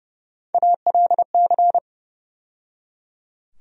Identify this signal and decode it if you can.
Morse code, heard as a keyed tone, about 30 wpm, 700 Hz. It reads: ALC
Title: USA Call Sign Postfixes - 3 Letters 30wpm